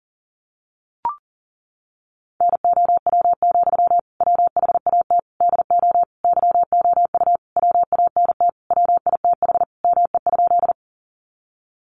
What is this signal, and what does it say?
Morse code, heard as a keyed tone, about 40 wpm, 700 Hz. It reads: NOW, WHAT DO YOU WANT WITH ME?
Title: Sentences from Top 200 Words 40wpm